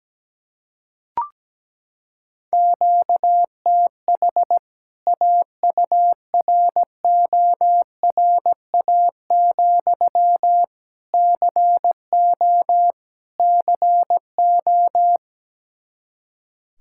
Morse code, heard as a keyed tone, about 17 wpm, 700 Hz. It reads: QTH AURORA, CO CO
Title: K1USN SST QSO Practice 17wpm